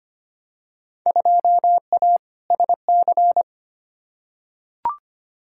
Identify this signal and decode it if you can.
Morse code, heard as a keyed tone, about 25 wpm, 700 Hz. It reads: 2A SC
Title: Field Day Exchanges - 25wpm